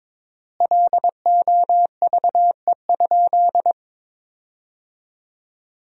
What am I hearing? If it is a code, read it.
Morse code, heard as a keyed tone, about 22 wpm, 700 Hz. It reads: LOVE?
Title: 4x Word-Spacing - Sentences from Top 500 Words - 22wpm